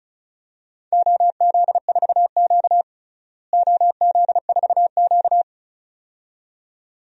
Morse code, heard as a keyed tone, about 36 wpm, 700 Hz. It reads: OZ4Q OZ4Q
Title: DXCC-100 Call Signs - 4 Characters - Repeated 2x - 35wpm